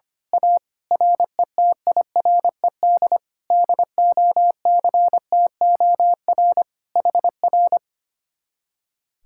Morse code, heard as a keyed tone, about 25 wpm, 700 Hz. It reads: A RETIRED DOCTOR HR
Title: Speed-Racing Common QSO Phrases - 25wpm